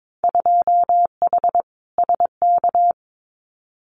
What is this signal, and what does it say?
Morse code, heard as a keyed tone, about 22 wpm, 700 Hz. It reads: 2H SK
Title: Winter Field Day Exchanges Encore 22wpm